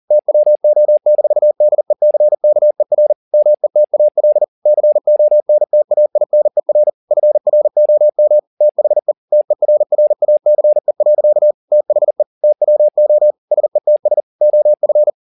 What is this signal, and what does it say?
Morse code, heard as a keyed tone, about 40 wpm, 600 Hz. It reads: TWO-DECKER METAL CONTAINER FROM THE TERRACE. THE TWO SETS OF